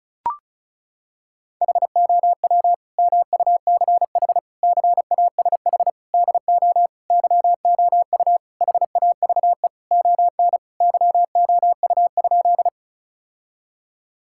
Morse code, heard as a keyed tone, about 34 wpm, 700 Hz. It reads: HOW MUCH CASH DO YOU HAVE ON YOU?
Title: Sentences from Top 1000 Words - Review Part 5 - 35wpm